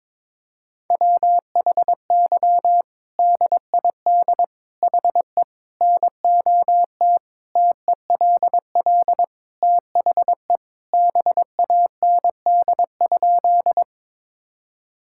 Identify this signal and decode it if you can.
Morse code, heard as a keyed tone, about 21 wpm, 700 Hz. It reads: WHY DID HE NOT TELL THE BAND?
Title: Sentences from Top 1600 Words 22wpm